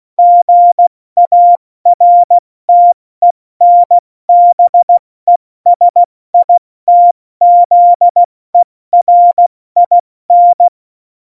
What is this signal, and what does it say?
Morse code, heard as a keyed tone, about 16 wpm, spaced as for 14 wpm, 700 Hz. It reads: GARTENBESITZERIN
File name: Morsekurs / Witze / Witz 07 April 2020, 70 F80